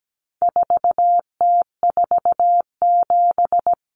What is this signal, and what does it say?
Morse code, heard as a keyed tone, about 17 wpm, 700 Hz. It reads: 4T47
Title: Common Cut Numbers - 4-Digit 17wpm